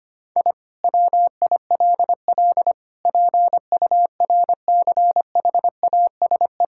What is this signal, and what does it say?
Morse code, heard as a keyed tone, about 25 wpm, 700 Hz. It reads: I WILL PURCHASE
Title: Sentences from Top 1000 Words - Review Part 5 - 25wpm